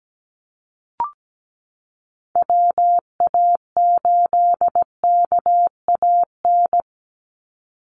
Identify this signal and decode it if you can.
Morse code, heard as a keyed tone, about 17 wpm, 700 Hz. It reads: WA8KAN
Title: Call Signs Repeated 3x - 17wpm